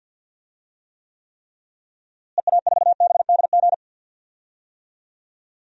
Morse code, heard as a keyed tone, about 46 wpm, 700 Hz. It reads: EA4BDG
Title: DXCC-100 Call Signs - 6 Characters - Repeated 3x - 50wpm